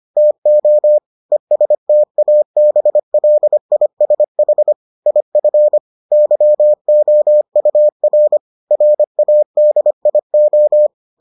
Morse code, heard as a keyed tone, about 25 wpm, 600 Hz. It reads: TO ESTABLISH IF YOUR RADIO